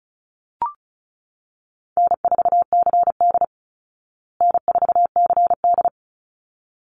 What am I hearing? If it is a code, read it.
Morse code, heard as a keyed tone, about 34 wpm, 700 Hz. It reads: N4CD N4CD
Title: Call Signs Repeated 3x - 35wpm